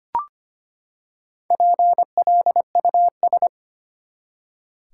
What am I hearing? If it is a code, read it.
Morse code, heard as a keyed tone, about 25 wpm, 700 Hz. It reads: PLUS
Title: Four Letter Words - 25wpm